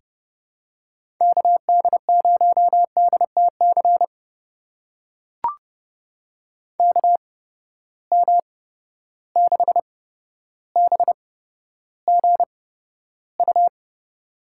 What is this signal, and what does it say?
Morse code, heard as a keyed tone, about 30 wpm, spaced as for 13 wpm, 700 Hz. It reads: KD0DTC KM6BGU
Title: USA Call Signs - 6 Characters - 8x Character Spacing - 30wpm